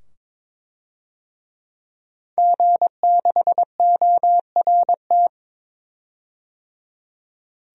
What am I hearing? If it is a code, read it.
Morse code, heard as a keyed tone, about 22 wpm, 700 Hz. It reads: G6ORT
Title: DXCC-100 Call Signs - 5 Characters - 22wpm